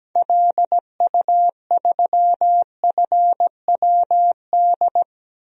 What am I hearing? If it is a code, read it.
Morse code, heard as a keyed tone, about 17 wpm, 700 Hz. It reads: LU3FWD
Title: DXCC-100 Call Signs - 6 Characters - 17wpm